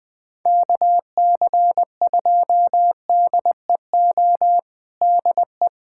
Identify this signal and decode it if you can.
Morse code, heard as a keyed tone, about 21 wpm, 700 Hz. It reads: KC2DEO DE
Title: USA Callsign DE Callsign 20wpm